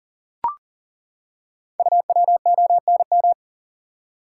Morse code, heard as a keyed tone, about 38 wpm, 690 Hz. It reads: UWONM
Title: Code Groups - 5 Letters - 40wpm